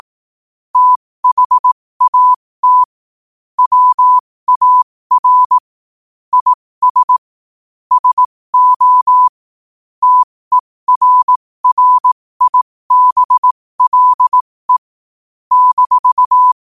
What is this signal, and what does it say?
Morse code, heard as a keyed tone, about 18 wpm, spaced as for 15 wpm, 1000 Hz. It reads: THAT WAR IS SO TERRIBLE -
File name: qotdfiles / 15wpm